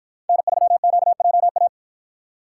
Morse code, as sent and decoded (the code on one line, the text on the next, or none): -. ...-- --.- .--- .-
N3QJA